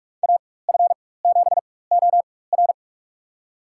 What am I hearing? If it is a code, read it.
Morse code, heard as a keyed tone, about 47 wpm, spaced as for 21 wpm, 700 Hz. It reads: AF7OR
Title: USA Call Signs - 4x Character Spacing - 45wpm